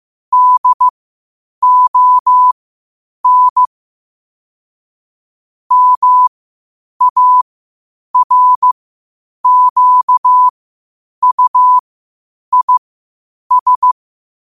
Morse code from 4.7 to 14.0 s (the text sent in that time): MARQUIS